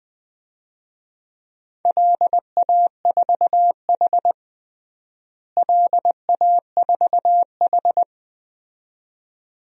Morse code, heard as a keyed tone, about 20 wpm, 700 Hz. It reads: LA4H LA4H
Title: DXCC-100 Call Signs - 4 Characters - Repeated 2x - 20wpm